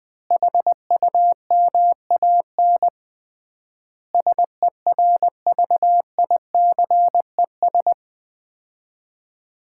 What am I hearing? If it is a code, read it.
Morse code, heard as a keyed tone, about 20 wpm, 700 Hz. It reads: HUMAN SERVICES
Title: 3x Word-Spacing - Sets of 5 Words - 20wpm